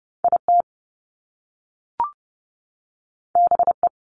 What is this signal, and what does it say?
Morse code, heard as a keyed tone, about 30 wpm, 700 Hz. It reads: IT BE
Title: Top Two Letter Combinations in Words 30wpm